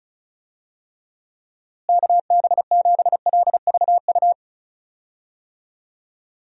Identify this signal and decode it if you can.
Morse code, heard as a keyed tone, about 34 wpm, 700 Hz. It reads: KB7LVU
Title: USA Call Signs - 6 Characters - Repeated 3x - 35wpm